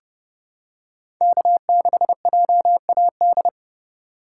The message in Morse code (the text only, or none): K6JAD